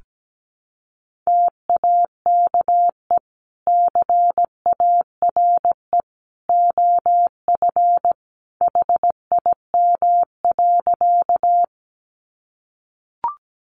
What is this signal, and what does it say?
Morse code, heard as a keyed tone, about 17 wpm, 700 Hz. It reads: TAKE CARE OF HIM.
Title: Sentences from Top 300 Words 17wpm